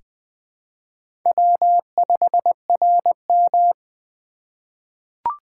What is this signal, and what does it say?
Morse code, heard as a keyed tone, about 20 wpm, 700 Hz. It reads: W5RM
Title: USA Call Signs - 4 Characters - 3x Character Spacing - 20wpm